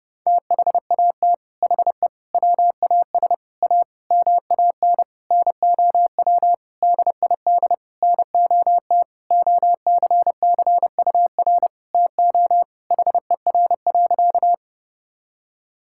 Morse code, heard as a keyed tone, about 30 wpm, 700 Hz. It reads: THAT HE WAS A MAN NOW DID NOT OCCUR TO HER.
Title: Sentences from Top 1000 Words - Review Part 6 - 30wpm